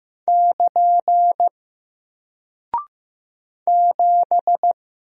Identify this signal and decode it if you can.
Morse code, heard as a keyed tone, about 15 wpm, 700 Hz. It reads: <KN> 7
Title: Single Character (Letter:Number:Punctuation:Prosign) 15wpm